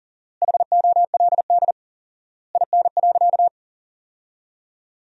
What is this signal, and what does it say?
Morse code, heard as a keyed tone, about 40 wpm, 700 Hz. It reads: HOLD IN.
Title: 4x Word-Spacing - Sentences from Top 400 Words - 40wpm